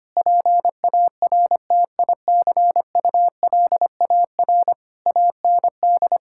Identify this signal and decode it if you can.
Morse code, heard as a keyed tone, about 25 wpm, 700 Hz. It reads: PARTICULAR AND
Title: Sentences from Top 1600 Words 25wpm